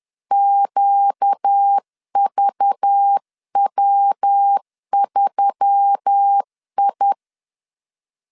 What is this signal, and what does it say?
Morse code, heard as a keyed tone, about 10 wpm, 800 Hz. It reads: QVW3I
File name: LEZIONI / 9 286 10wpm 2735